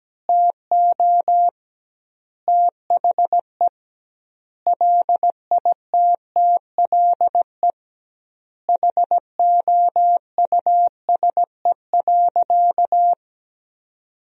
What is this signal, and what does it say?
Morse code, heard as a keyed tone, about 17 wpm, 700 Hz. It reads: TO THE LITTLE HOUSE.
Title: 2x Word-Spacing - Sentences from Top 200 Words - 17wpm